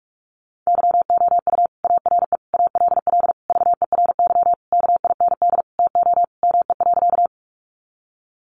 Morse code, heard as a keyed tone, about 44 wpm, 700 Hz. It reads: YOU ARE ALL VERY KIND TO ME.
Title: Sentences from Top 200 Words 45wpm